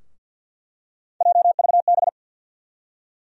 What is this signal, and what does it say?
Morse code, heard as a keyed tone, about 46 wpm, 700 Hz. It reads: JVB
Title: USA Call Sign Postfixes - 3 Letters 50wpm